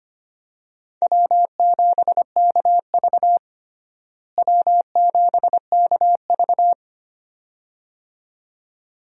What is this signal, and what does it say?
Morse code, heard as a keyed tone, about 25 wpm, 700 Hz. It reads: W7KV W7KV
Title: USA Call Signs - 4 Characters - Repeated 2x - 25wpm